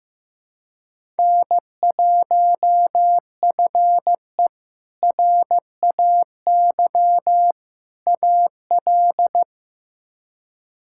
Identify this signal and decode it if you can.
Morse code, heard as a keyed tone, about 15 wpm, 700 Hz. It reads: N1FE RAY AL
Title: K1USN SST QSO Practice - 15wpm